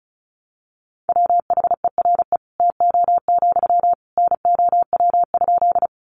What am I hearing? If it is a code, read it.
Morse code, heard as a keyed tone, about 34 wpm, 700 Hz. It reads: WHERE TO, NOW?
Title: Sentences from Top 200 Words 35wpm